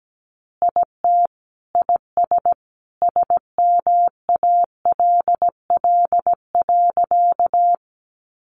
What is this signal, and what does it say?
Morse code, heard as a keyed tone, about 17 wpm, 700 Hz. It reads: IT IS SMALL.
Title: Additional Easy Phrases 17wpm